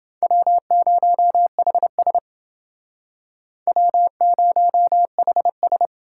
Morse code, heard as a keyed tone, about 29 wpm, 700 Hz. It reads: W0HS W0HS